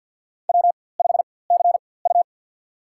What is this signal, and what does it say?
Morse code, heard as a keyed tone, about 53 wpm, spaced as for 24 wpm, 700 Hz. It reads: W5CU
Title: USA Call Signs - 4x Character Spacing - 50wpm